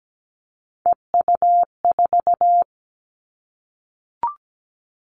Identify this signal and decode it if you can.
Morse code, heard as a keyed tone, about 17 wpm, 700 Hz. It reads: EU4
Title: Callsign Prefixes for Top 50 DXCC - 17wpm